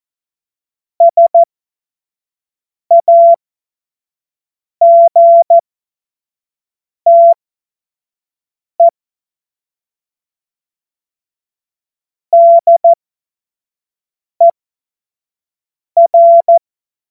Morse code, heard as a keyed tone, about 14 wpm, spaced as for 5 wpm, 680 Hz. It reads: SAGTE DER
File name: Morsekurs / Witze / Witz 16. Dezember 2025 25 F70